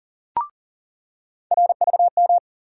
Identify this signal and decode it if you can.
Morse code, heard as a keyed tone, about 40 wpm, 700 Hz. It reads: RVM
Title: USA Call Sign Postfixes - 3 Letters 40wpm